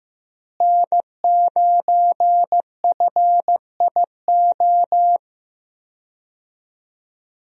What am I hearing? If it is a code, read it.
Morse code, heard as a keyed tone, about 15 wpm, 700 Hz. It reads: N9FIO
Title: USA Call Signs - 5 Characters - Repeated 3x - 15wpm